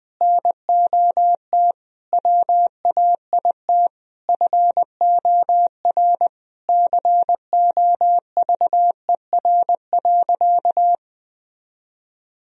Morse code, heard as a keyed tone, about 20 wpm, 700 Hz. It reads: NOT WAIT FOR COVER.
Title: Sentences from Top 1000 Words - Review Part 1 - 20wpm